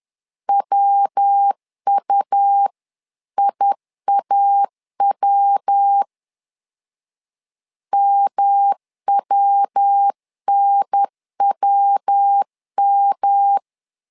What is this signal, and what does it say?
Morse code, heard as a keyed tone, about 10 wpm, 800 Hz. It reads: WUIAW MWNWM
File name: LEZIONI / 3 MWRO / 3 MW 10wpm 1959